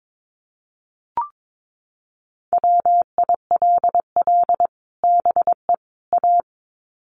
Morse code, heard as a keyed tone, about 22 wpm, 700 Hz. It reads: WILL BE A
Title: Sets of 3 Words 22wpm